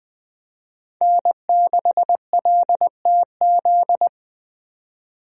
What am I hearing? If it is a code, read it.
Morse code, heard as a keyed tone, about 20 wpm, 700 Hz. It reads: N6LTZ
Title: USA Call Signs - 5 Characters - Repeated 3x - 20wpm